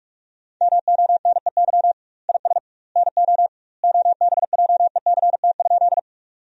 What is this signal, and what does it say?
Morse code, heard as a keyed tone, about 46 wpm, 700 Hz. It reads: MONEY IS NO OBJECT?
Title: 2x Word-Spacing - Sentences from Top 500 Words - 45wpm